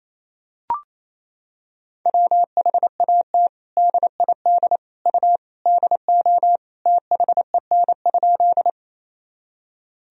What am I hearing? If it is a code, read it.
Morse code, heard as a keyed tone, about 28 wpm, 700 Hz. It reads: WHAT DID U DO THEN?
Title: Sentences from Top 100 Words and Abbreviations 28wpm